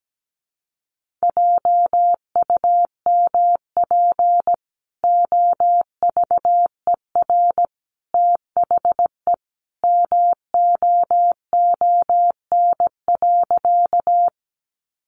Morse code, heard as a keyed tone, about 17 wpm, 700 Hz. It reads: JUMP OVER THE MOON.
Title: Sentences from Top 1300 Words 17wpm